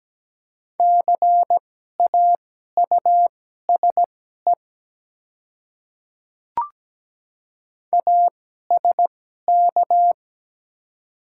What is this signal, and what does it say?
Morse code, heard as a keyed tone, about 17 wpm, spaced as for 12 wpm, 700 Hz. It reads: CAUSE ASK
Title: 2x Character Spacing - Top 200 Words - 17wpm